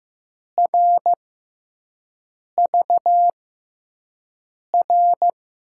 Morse code, heard as a keyed tone, about 15 wpm, spaced as for 5 wpm, 700 Hz. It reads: RVR
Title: USA Call Signs - 5 Characters - 6x Character Spacing - 15wpm